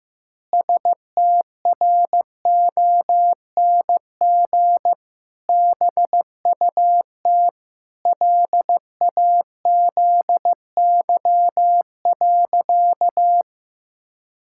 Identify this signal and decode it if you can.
Morse code, heard as a keyed tone, about 15 wpm, 700 Hz. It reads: STRONG BUT LAZY.
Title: Pangram Challenge v2 - 15wpm